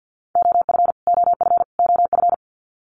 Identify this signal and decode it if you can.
Morse code, heard as a keyed tone, about 49 wpm, 700 Hz. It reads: OF OF OF
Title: Top 500 Words Repeated 50wpm